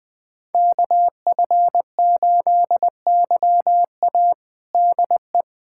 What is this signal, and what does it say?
Morse code, heard as a keyed tone, about 21 wpm, 700 Hz. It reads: KF8YA DE
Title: USA Callsign DE Callsign - 5 Characters - 20wpm